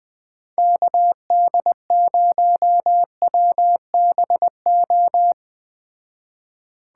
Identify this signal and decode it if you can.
Morse code, heard as a keyed tone, about 20 wpm, 700 Hz. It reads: KD0WBO